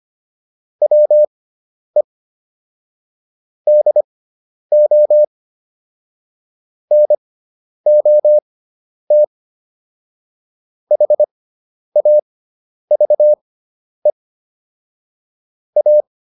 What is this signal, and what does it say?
Morse code, heard as a keyed tone, about 25 wpm, spaced as for 10 wpm, 600 Hz. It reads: WE DO NOT HAVE A